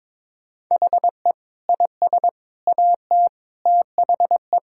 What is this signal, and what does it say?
Morse code, heard as a keyed tone, about 21 wpm, 700 Hz. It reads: HE IS AT THE
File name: practice / 22wpm / Sentences with 4-Word Phrases - 22wpm